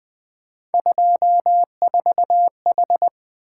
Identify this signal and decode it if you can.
Morse code, heard as a keyed tone, about 20 wpm, 700 Hz. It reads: 24H
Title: Three Letter-Number Combinations - 20wpm